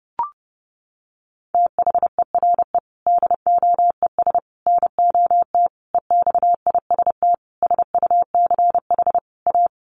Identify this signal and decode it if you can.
Morse code, heard as a keyed tone, about 30 wpm, 700 Hz. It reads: THERE DOES NOT EXIST SUCH A
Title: Sentences from Top 1400 Words 30wpm